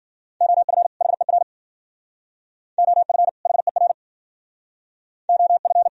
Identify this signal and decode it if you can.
Morse code, heard as a keyed tone, about 51 wpm, 700 Hz. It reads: OF HER OF HER OF